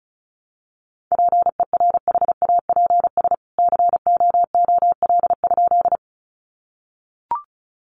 Morse code, heard as a keyed tone, about 34 wpm, 700 Hz. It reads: PERHAPS COOL?
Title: Sentences from Top 1200 Words 35wpm